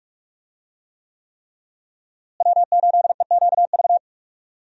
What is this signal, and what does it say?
Morse code, heard as a keyed tone, about 42 wpm, 700 Hz. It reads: W8EQV
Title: USA Call Signs - 5 Characters - Repeated 3x - 45wpm